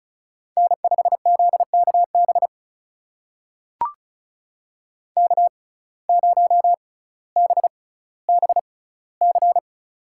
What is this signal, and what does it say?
Morse code, heard as a keyed tone, about 36 wpm, spaced as for 18 wpm, 700 Hz. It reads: N5ZKB K0BBC
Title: USA Call Signs - 5 Characters - 6x Character Spacing - 35wpm